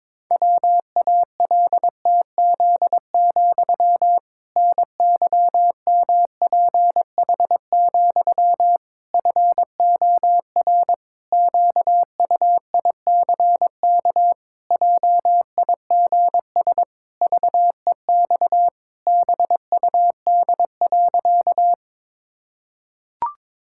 Morse code, heard as a keyed tone, about 22 wpm, 700 Hz. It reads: WALTZ, NYMPH, FOR QUICK JIGS VEX BUD.
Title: Pangram Challenge v2 - 22wpm